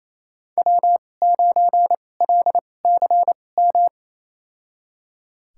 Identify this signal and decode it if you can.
Morse code, heard as a keyed tone, about 27 wpm, spaced as for 20 wpm, 700 Hz. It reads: W9LCM
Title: USA Call Signs - 5 Characters - 2x Character Spacing - 28wpm